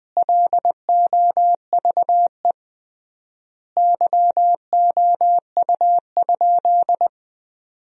Morse code, heard as a keyed tone, about 20 wpm, 700 Hz. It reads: LOVE YOU?
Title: 3x Word-Spacing - Sentences from Top 400 Words - 20wpm